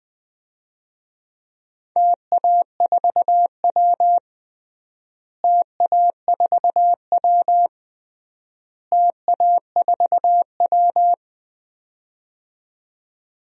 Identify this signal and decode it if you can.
Morse code, heard as a keyed tone, about 20 wpm, 700 Hz. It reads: TA4W TA4W TA4W
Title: DXCC-100 Call Signs - 4 Characters - Repeated 3x - 20wpm